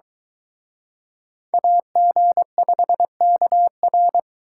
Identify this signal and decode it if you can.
Morse code, heard as a keyed tone, about 23 wpm, 700 Hz. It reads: AG5KR